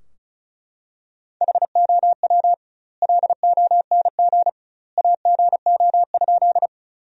Morse code, heard as a keyed tone, about 34 wpm, 700 Hz. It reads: HOW LONG AGO?
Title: 2x Word-Spacing - Sentences from Top 500 Words - 35wpm